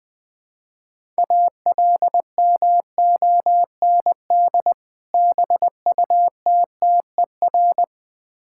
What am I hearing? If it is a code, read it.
Morse code, heard as a keyed tone, about 20 wpm, 700 Hz. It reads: ALMOND BUTTER
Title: Difficult Words with Hints - 20wpm